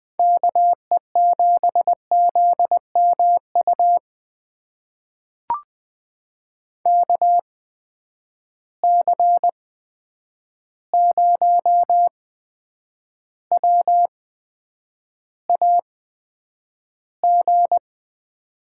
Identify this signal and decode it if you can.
Morse code, heard as a keyed tone, about 20 wpm, spaced as for 9 wpm, 700 Hz. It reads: KE7ZMU KC0WAG
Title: USA Call Signs - 8x Character Spacing - 20wpm